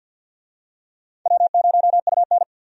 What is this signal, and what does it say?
Morse code, heard as a keyed tone, about 52 wpm, 700 Hz. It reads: W0UN
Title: USA Call Signs - 4 Characters - 50wpm